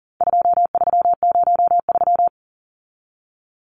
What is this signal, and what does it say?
Morse code, heard as a keyed tone, about 40 wpm, 700 Hz. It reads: 2303
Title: Numbers - 4-Digit 40wpm